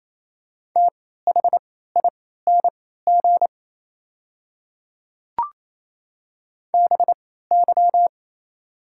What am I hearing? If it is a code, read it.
Morse code, heard as a keyed tone, about 28 wpm, spaced as for 16 wpm, 700 Hz. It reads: THING BY
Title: 3x Character Spacing - Top 100 Words - 28wpm